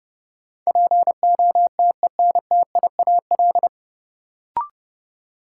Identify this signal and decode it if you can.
Morse code, heard as a keyed tone, about 30 wpm, 700 Hz. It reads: POTENTIAL
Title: Sentences from Top 1000 Words - Review Part 4 - New Word Drill - 30wpm